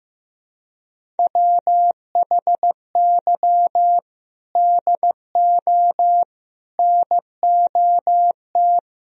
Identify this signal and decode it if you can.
Morse code, heard as a keyed tone, about 15 wpm, 700 Hz. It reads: WHY DO NOT
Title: Speed-Racing - Sets of 5 Words 15wpm